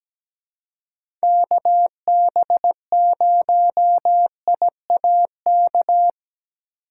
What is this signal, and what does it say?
Morse code, heard as a keyed tone, about 17 wpm, 700 Hz. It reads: KB0IAK
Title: USA Call Signs - 6 Characters - 4x Character Spacing - 17wpm